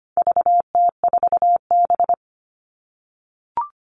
Morse code, heard as a keyed tone, about 25 wpm, 700 Hz. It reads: VT4B
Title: All Cut Numbers - 4-Digit 25wpm